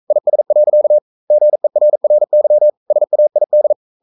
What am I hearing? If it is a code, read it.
Morse code, heard as a keyed tone, about 42 wpm, 600 Hz. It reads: IS. GERRY SAID
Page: Romanen Never af Ken Follet på engelsk.